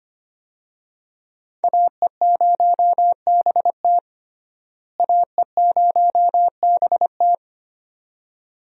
Morse code, heard as a keyed tone, about 25 wpm, 700 Hz. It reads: AE0BT AE0BT
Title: USA Call Signs - 5 Characters - Repeated 2x - 25wpm